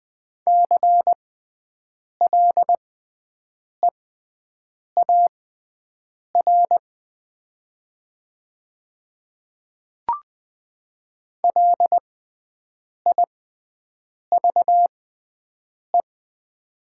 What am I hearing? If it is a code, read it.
Morse code, heard as a keyed tone, about 20 wpm, spaced as for 7 wpm, 700 Hz. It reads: CLEAR LIVE
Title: 6x Character Spacing - Top 500 Words - 20wpm